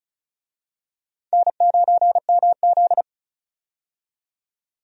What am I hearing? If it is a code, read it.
Morse code, heard as a keyed tone, about 36 wpm, 700 Hz. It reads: N9MZ
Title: USA Call Signs - 4 Characters - 35wpm